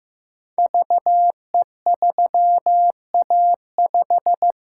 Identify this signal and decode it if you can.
Morse code, heard as a keyed tone, about 15 wpm, 700 Hz. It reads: VE3A5
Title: Code Groups - 5 Letter:Numbers - 15wpm